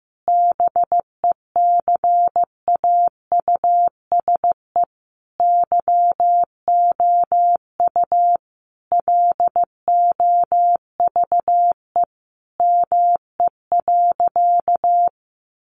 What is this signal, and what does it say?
Morse code, heard as a keyed tone, about 15 wpm, 700 Hz. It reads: BECAUSE YOU LOVE ME.
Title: Sentences from Top 600 Words 15wpm